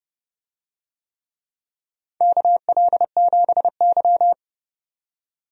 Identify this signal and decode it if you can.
Morse code, heard as a keyed tone, about 30 wpm, 700 Hz. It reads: KL7Y
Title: DXCC-100 Call Signs - 4 Characters - 30wpm